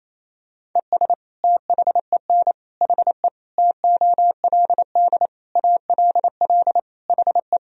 Morse code, heard as a keyed tone, about 27 wpm, 700 Hz. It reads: ES THEN HE TOLD ALL HE
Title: Sentences from Top 400 Words and Abbreviations 28wpm